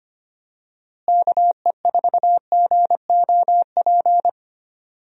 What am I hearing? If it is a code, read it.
Morse code, heard as a keyed tone, about 25 wpm, 700 Hz. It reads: KE4GOP
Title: USA Call Signs - 6 Characters - Repeated 3x - 25wpm